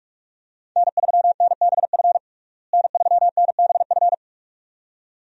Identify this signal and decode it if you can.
Morse code, heard as a keyed tone, about 47 wpm, 700 Hz. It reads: N3NBF N3NBF
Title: USA Call Signs - 5 Characters - Repeated 2x - 45wpm